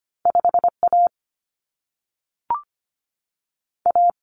Morse code, heard as a keyed tone, about 25 wpm, 700 Hz. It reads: 5 A A
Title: Common Cut Numbers - 2-Digit 25wpm